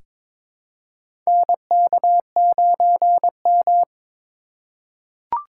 Morse code, heard as a keyed tone, about 22 wpm, 700 Hz. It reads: NK9M
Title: USA Call Signs - 4x Character Spacing - 22wpm